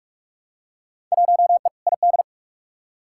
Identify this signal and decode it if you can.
Morse code, heard as a keyed tone, about 46 wpm, 700 Hz. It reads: 1E ID
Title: Field Day Exchanges - 45wpm